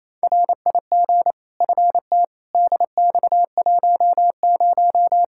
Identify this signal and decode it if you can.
Morse code, heard as a keyed tone, about 28 wpm, 700 Hz. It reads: RIG FT DX10
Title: Practice QSO - Exchange 2 - Info - 28wpm